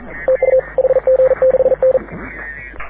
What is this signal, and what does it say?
Morse code, heard as a keyed tone, about 41 wpm, 530 Hz. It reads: TU HG6N
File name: AUDIO2019 / HG6N CW SSB MIX